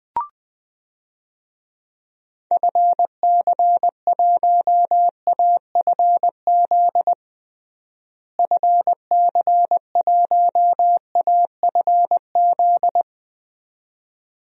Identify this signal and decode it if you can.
Morse code, heard as a keyed tone, about 20 wpm, 700 Hz. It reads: FC1AFZ FC1AFZ
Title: DXCC-100 Call Signs - 6 Characters - Repeated 2x - 20wpm